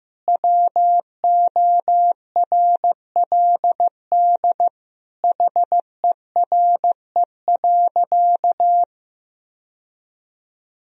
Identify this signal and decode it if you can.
Morse code, heard as a keyed tone, about 15 wpm, 700 Hz. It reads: WORLD HERE.
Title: Sentences from Top 1000 Words - Review Part 3 - 15wpm